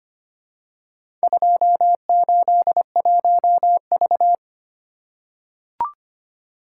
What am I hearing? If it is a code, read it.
Morse code, heard as a keyed tone, about 25 wpm, 700 Hz. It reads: 281V
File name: practice / 25wpm / Four Letter-Number Combinations - 25wpm